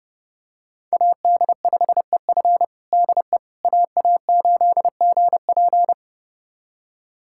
Morse code, heard as a keyed tone, about 29 wpm, 700 Hz. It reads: AD5EF DE AA8GP
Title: USA Callsign DE Callsign - 5 Characters - 30wpm